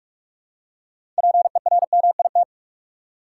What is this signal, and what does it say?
Morse code, heard as a keyed tone, about 46 wpm, 700 Hz. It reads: PERMIT
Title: Six Letter Words - 45wpm